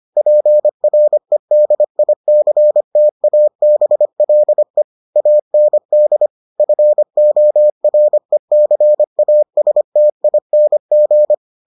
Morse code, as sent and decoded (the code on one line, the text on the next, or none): .--. .-. -.. .. -.-. - .- -... .-.. / .- -. -.. / ..-. --- .-. -.-. .- ... - .. -. --.
PREDICTABLE AND FORECASTING